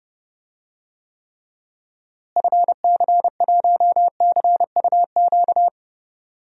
Morse code, heard as a keyed tone, about 30 wpm, 700 Hz. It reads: FC1CUQ
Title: DXCC-100 Call Signs - 6 Characters - Repeated 3x - 30wpm